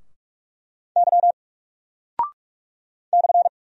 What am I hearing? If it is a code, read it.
Morse code, heard as a keyed tone, about 44 wpm, 700 Hz. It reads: Y /
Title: Single Character (Letter:Number:Punctuation:Prosign) Alt 45wpm